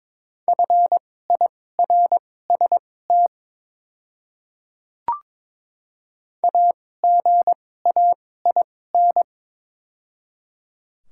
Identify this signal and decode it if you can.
Morse code, heard as a keyed tone, about 21 wpm, spaced as for 16 wpm, 700 Hz. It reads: FIRST AGAIN
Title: 2x Character Spacing - Top 500 Words - 22wpm